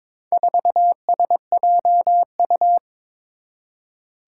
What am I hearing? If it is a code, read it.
Morse code, heard as a keyed tone, about 22 wpm, 700 Hz. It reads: 4SJU
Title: Four Challenging Letter-Number Combinations 22wpm